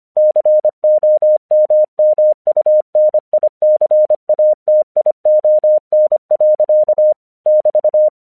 Morse code, heard as a keyed tone, about 25 wpm, 610 Hz. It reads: COMMUNICATION. <BT>
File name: 1 English 25wpm / AQC05